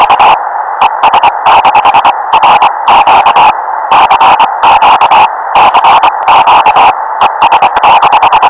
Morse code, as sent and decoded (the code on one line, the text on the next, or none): none